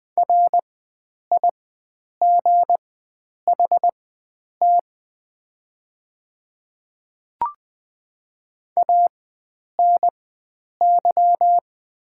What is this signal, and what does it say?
Morse code, heard as a keyed tone, about 20 wpm, spaced as for 9 wpm, 700 Hz. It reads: RIGHT ANY
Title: 4x Character Spacing - Top 300 Words - 20wpm